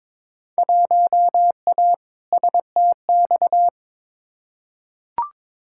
Morse code, heard as a keyed tone, about 22 wpm, 700 Hz. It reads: 1A STX
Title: Field Day Exchanges Encore - 22wpm